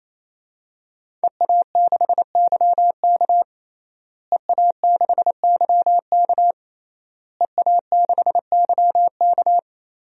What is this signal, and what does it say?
Morse code, heard as a keyed tone, about 28 wpm, 700 Hz. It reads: EA6YK EA6YK EA6YK
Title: DXCC-100 Call Signs - 5 Characters - Repeated 3x - 28wpm